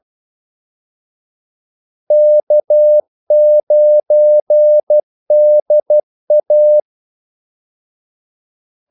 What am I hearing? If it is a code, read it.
Morse code, heard as a keyed tone, about 12 wpm, 600 Hz. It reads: K9DA
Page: Kaldesignaler 12 wpm